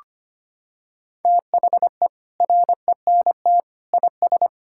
Morse code, heard as a keyed tone, about 25 wpm, 700 Hz. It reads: THE RENT IS
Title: Sentences from Top 1000 Words - Review Part 5 - 25wpm